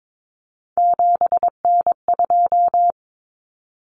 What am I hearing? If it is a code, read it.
Morse code, heard as a keyed tone, about 22 wpm, 700 Hz. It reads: 7N2
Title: Common Cut Numbers - 3-Digit 22wpm